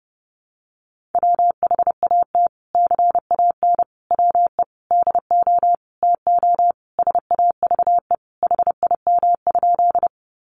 Morse code, heard as a keyed tone, about 30 wpm, 700 Hz. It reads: WHAT CAN WE DO TO SAVE HIM?
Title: Sentences from Top 1100 Words 30wpm